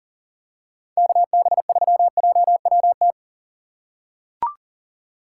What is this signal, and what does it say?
Morse code, heard as a keyed tone, about 38 wpm, 700 Hz. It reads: KB3JWT
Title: USA Call Signs - 6 Characters - 3x Character Spacing - 40wpm